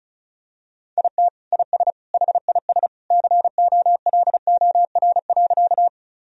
Morse code, heard as a keyed tone, about 34 wpm, 700 Hz. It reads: IT IS HIS COLOR.
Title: Sentences from Top 1000 Words - Review Part 2 - 35wpm